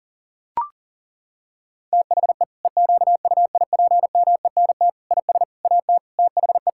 Morse code, heard as a keyed tone, about 40 wpm, 700 Hz. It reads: THE EQUIPMENT IS AT THE
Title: Sentences from Top 1000 Words - Review Part 5 - 40wpm